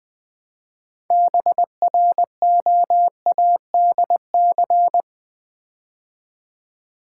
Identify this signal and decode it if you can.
Morse code, heard as a keyed tone, about 20 wpm, 700 Hz. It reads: BROADC
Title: Build A Word - 3 Letters At A Time - 9-Letter Words - 20wpm